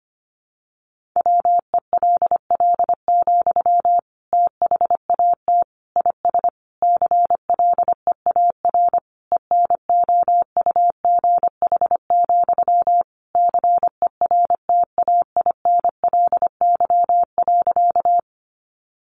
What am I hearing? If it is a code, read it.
Morse code, heard as a keyed tone, about 25 wpm, 700 Hz. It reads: WELL, THAT IS CLEAR ENOUGH, CERTAINLY.
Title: Sentences from Top 900 Words 25wpm